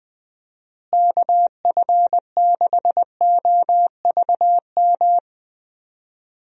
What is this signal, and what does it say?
Morse code, heard as a keyed tone, about 20 wpm, 700 Hz. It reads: KF6OVM
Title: USA Call Signs - 6 Characters - Repeated 2x - 20wpm